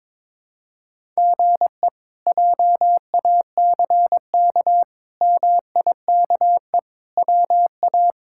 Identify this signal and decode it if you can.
Morse code, heard as a keyed tone, about 22 wpm, 700 Hz. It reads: GE JACK MIKE WA
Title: K1USN SST QSO Practice - 22wpm